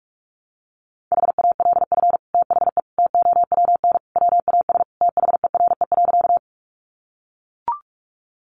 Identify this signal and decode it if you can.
Morse code, heard as a keyed tone, about 44 wpm, 700 Hz. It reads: HALF THE TOWN WAS THERE.
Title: Sentences from Top 500 Words 45wpm